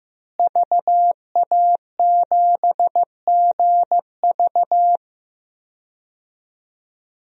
Morse code, heard as a keyed tone, about 15 wpm, 700 Hz. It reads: VA7GV
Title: DXCC-100 Call Signs - 5 Characters - Repeated 2x - 15wpm